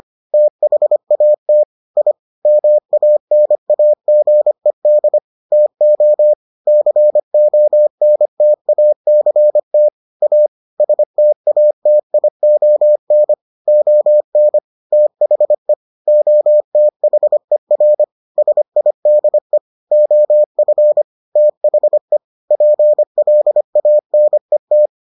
Morse code, heard as a keyed tone, about 25 wpm, 600 Hz. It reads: THAT I MANAGED TO CONTACT A STATION ON THE OTHER SIDE OF THE PLANET